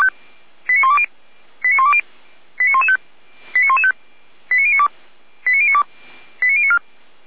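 Five-tone selective call (ZVEI), heard as quick runs of five tones, tones of about 70 ms. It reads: 89120 89120 89115 89115 89092 89092 89094
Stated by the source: ZVEI five-tone selective call received by SDR.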